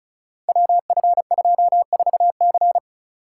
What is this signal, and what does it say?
Morse code, heard as a keyed tone, about 36 wpm, 700 Hz. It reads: WF24C
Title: Code Groups - 5 Letter:Numbers - 35wpm